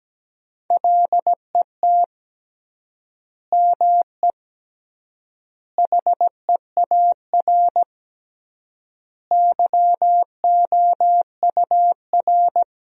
Morse code, heard as a keyed tone, about 17 wpm, 700 Hz. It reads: LET ME HEAR YOUR
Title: 3x Word-Spacing - Sentences from Top 300 Words - 17wpm